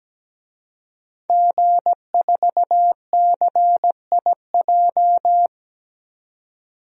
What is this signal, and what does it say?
Morse code, heard as a keyed tone, about 17 wpm, 700 Hz. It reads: G4CIJ
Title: DXCC-100 Call Signs - 5 Characters - 17wpm